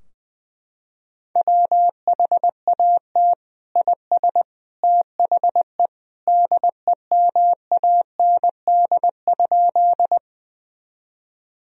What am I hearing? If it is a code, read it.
Morse code, heard as a keyed tone, about 21 wpm, 700 Hz. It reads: WHAT IS THE DEMAND?
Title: Sentences from Top 1000 Words - Review Part 4 - 20wpm